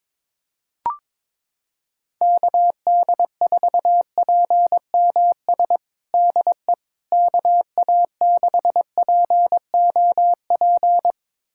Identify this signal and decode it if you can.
Morse code, heard as a keyed tone, about 21 wpm, 700 Hz. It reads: KD4PMS DE KA6POP
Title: USA Callsign DE Callsign 22wpm